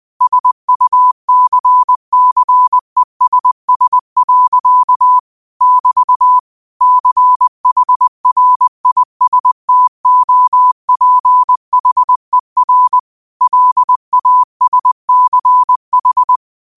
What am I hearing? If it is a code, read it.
Morse code, heard as a keyed tone, about 20 wpm, 1000 Hz. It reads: SUCCESS. <BT> CHRISTOPHER LASCH